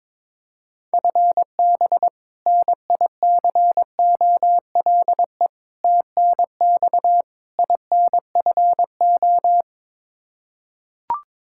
Morse code, heard as a keyed tone, about 22 wpm, 700 Hz. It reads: FB NICOLE TNX INFO
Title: Practice QSO - Exchange 3 - More Info - 22wpm